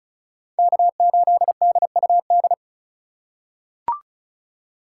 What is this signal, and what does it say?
Morse code, heard as a keyed tone, about 36 wpm, 700 Hz. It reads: K8DUD
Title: USA Call Signs - 3x Character Spacing - 35wpm